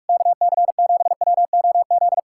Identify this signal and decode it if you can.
Morse code, heard as a keyed tone, about 68 wpm, spaced as for 56 wpm, 700 Hz. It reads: KC7WOZ
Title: Speed-Racing Common QSO Phrases Encore - 45wpm